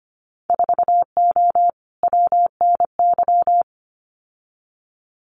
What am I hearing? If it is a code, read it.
Morse code, heard as a keyed tone, about 25 wpm, 690 Hz. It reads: 4O WNY
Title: Winter Field Day Exchanges Encore 25wpm